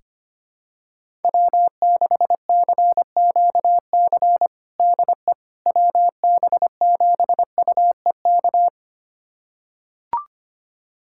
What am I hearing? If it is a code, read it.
Morse code, heard as a keyed tone, about 25 wpm, 700 Hz. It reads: W6CQC DE WB7UEK
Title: USA Callsign DE Callsign 25wpm